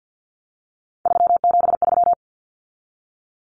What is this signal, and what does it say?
Morse code, heard as a keyed tone, about 50 wpm, 700 Hz. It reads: 373
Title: Numbers - 3-Digit 50wpm